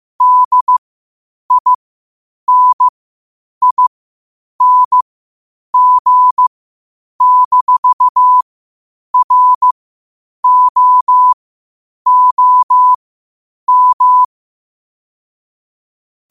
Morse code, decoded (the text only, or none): DINING-ROOM